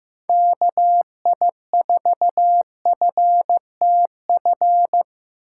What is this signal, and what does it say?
Morse code, heard as a keyed tone, about 15 wpm, 700 Hz. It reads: KI4FTF
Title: USA Call Signs - 2x Character Spacing - 15wpm